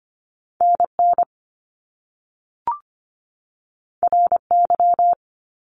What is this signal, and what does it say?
Morse code, heard as a keyed tone, about 25 wpm, 700 Hz. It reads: NN RY
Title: Top Two Letter Combinations in Words 25wpm